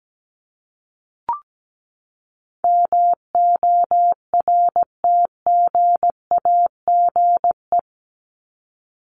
Morse code, heard as a keyed tone, about 17 wpm, 700 Hz. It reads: MORTGAGE
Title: Top 1751-2000 Words 17wpm